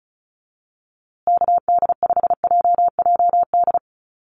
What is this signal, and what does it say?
Morse code, heard as a keyed tone, about 34 wpm, 700 Hz. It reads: KD5JJD
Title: USA Call Signs Repeated 3x 35wpm